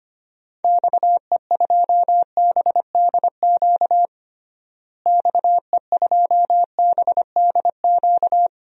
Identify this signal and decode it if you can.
Morse code, heard as a keyed tone, about 25 wpm, 700 Hz. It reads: XE2BDQ XE2BDQ
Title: DXCC-100 Call Signs - 6 Characters - Repeated 2x - 25wpm